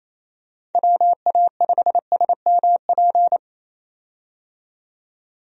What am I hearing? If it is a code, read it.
Morse code, heard as a keyed tone, about 28 wpm, 700 Hz. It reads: WA5SMP
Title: USA Call Signs - 6 Characters - 28wpm